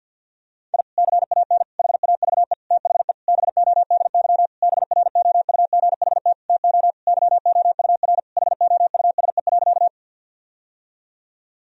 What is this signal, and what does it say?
Morse code, heard as a keyed tone, about 46 wpm, 700 Hz. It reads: I CAN HAVE THE BODY BROUGHT TO YOUR HOUSE.
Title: Sentences from Top 1000 Words - Review Part 2 - 50wpm